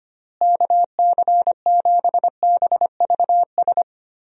Morse code, heard as a keyed tone, about 25 wpm, 700 Hz. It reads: KC7BVS
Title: USA Call Signs - 6 Characters - Repeated 2x - 25wpm